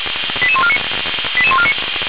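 Five-tone selective call (ZVEI), heard as quick runs of five tones, tones of about 70 ms. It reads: 88149 88149
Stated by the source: Five-tone ZVEI selective call, received on an SDR at 455.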